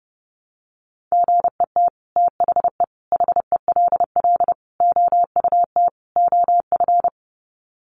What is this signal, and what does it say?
Morse code, heard as a keyed tone, about 30 wpm, 700 Hz. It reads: GET THE HELL OUT OF
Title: Sets of 5 Words Encore2 30wpm